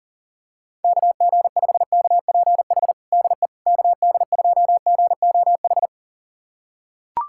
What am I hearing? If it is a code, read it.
Morse code, heard as a keyed tone, about 40 wpm, 700 Hz. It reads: KG5KPH DE KD2GOH
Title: USA Callsign DE Callsign 40wpm